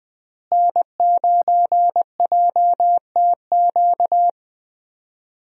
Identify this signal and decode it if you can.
Morse code, heard as a keyed tone, about 20 wpm, 700 Hz. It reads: N9JTQ
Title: USA Call Signs - 5 Characters - Repeated 3x - 20wpm